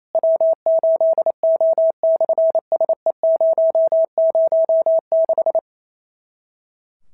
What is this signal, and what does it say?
Morse code, heard as a keyed tone, about 28 wpm, 650 Hz. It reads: W8O/SE006
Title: SOTA QSO Practice 28wpm